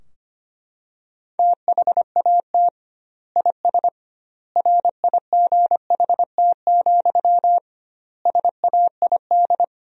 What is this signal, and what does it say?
Morse code, heard as a keyed tone, about 25 wpm, 700 Hz. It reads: THAT IS RIGHT, SAID
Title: 2x Word-Spacing - Sentences from Top 200 Words - 25wpm